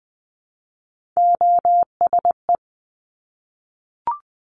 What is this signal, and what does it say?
Morse code, heard as a keyed tone, about 19 wpm, 700 Hz. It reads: OSE
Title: Top Three Letter Combinations in Words 20wpm